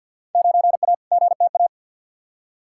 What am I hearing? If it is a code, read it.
Morse code, heard as a keyed tone, about 53 wpm, 700 Hz. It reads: 9A GTA
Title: Field Day Exchanges Encore - 50wpm